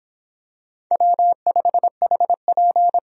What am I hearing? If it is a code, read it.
Morse code, heard as a keyed tone, about 24 wpm, 700 Hz. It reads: W5HP